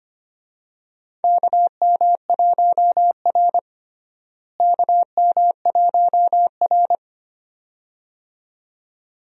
Morse code, heard as a keyed tone, about 25 wpm, 700 Hz. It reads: KM1R KM1R
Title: USA Call Signs - 4 Characters - Repeated 2x - 25wpm